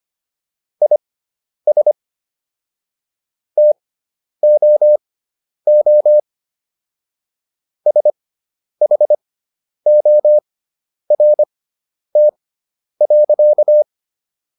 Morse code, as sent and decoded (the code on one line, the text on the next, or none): .. ... / - --- --- / ... .... --- .-. - .-.-.-
IS TOO SHORT.